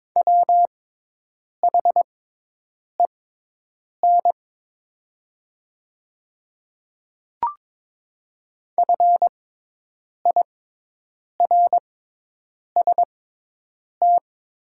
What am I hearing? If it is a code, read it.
Morse code, heard as a keyed tone, about 22 wpm, spaced as for 8 wpm, 700 Hz. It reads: WHEN FIRST
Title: 6x Character Spacing - Top 300 Words - 22wpm